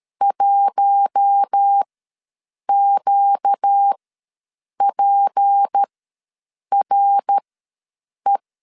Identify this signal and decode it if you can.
Morse code, heard as a keyed tone, about 13 wpm, spaced as for 7 wpm, 800 Hz. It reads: 1QPRE
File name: LEZIONI / 6 PKX0 / 6 PK ws 12wpm 3116